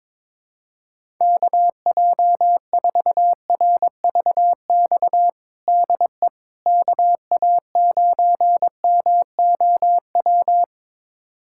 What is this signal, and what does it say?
Morse code, heard as a keyed tone, about 22 wpm, 700 Hz. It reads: KJ4RVX DE KA9MOW
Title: USA Callsign DE Callsign - 6 Characters - 22wpm